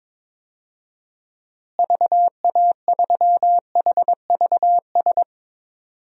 Morse code, heard as a keyed tone, about 22 wpm, 700 Hz. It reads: VA3HVS
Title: DXCC-100 Call Signs - 6 Characters - Repeated 2x - 22wpm